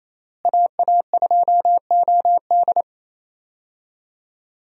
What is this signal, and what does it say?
Morse code, heard as a keyed tone, about 27 wpm, 700 Hz. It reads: AA2OD
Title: USA Call Signs - 5 Characters - Repeated 3x - 28wpm